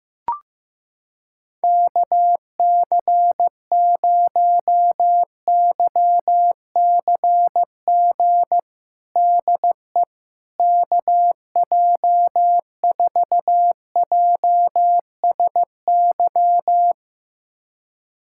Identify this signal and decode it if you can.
Morse code, heard as a keyed tone, about 15 wpm, 700 Hz. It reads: KC0YCG DE KJ4JSY
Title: USA Callsign DE Callsign - 6 Characters - 15wpm